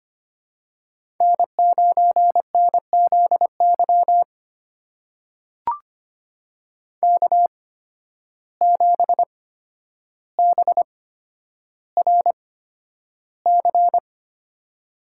Morse code, heard as a keyed tone, about 25 wpm, spaced as for 11 wpm, 700 Hz. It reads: N9NZY K7BRC
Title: USA Call Signs - 8x Character Spacing - 25wpm